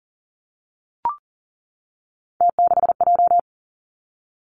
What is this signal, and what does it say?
Morse code, heard as a keyed tone, about 40 wpm, 700 Hz. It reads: T6J
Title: Three Challenging Letter-Number Combinations 40wpm